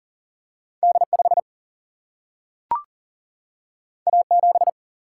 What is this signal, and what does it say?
Morse code, heard as a keyed tone, about 40 wpm, 700 Hz. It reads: D 5 A 7
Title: Single-Letter Single-Number 40wpm